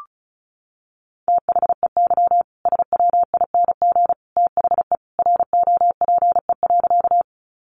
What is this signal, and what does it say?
Morse code, heard as a keyed tone, about 34 wpm, 700 Hz. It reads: THEY SWING THE ROPE.
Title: Easy Phrases 35wpm